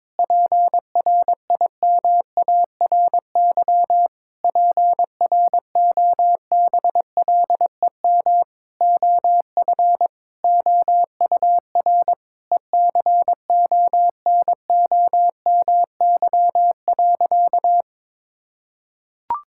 Morse code, heard as a keyed tone, about 22 wpm, 700 Hz. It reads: PRIMARY PROBLEM OF OUR ECONOMY.
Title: Sentences from Top 1000 Words - Review Part 5 - 22wpm